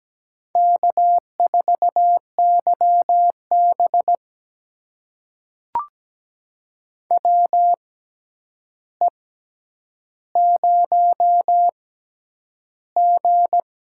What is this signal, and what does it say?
Morse code, heard as a keyed tone, about 17 wpm, spaced as for 9 wpm, 700 Hz. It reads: K4YB WE0G
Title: USA Call Signs - 4 Characters - 6x Character Spacing - 17wpm